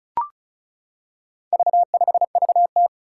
Morse code, heard as a keyed tone, about 36 wpm, 700 Hz. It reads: V5VT
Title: Four Challenging Letter-Number Combinations 35wpm